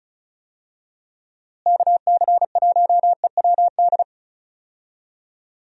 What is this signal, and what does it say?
Morse code, heard as a keyed tone, about 36 wpm, 700 Hz. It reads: KC1EWD
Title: USA Call Signs - 6 Characters - Repeated 2x - 35wpm